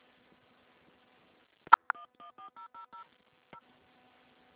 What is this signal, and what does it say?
DTMF, heard as telephone touch tones, tones of about 110 ms. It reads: 114000 0